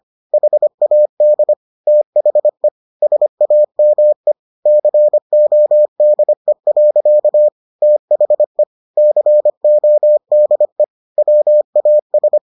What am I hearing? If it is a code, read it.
Morse code, heard as a keyed tone, about 25 wpm, 600 Hz. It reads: HAD THE SAME CODE. THE CODE WAS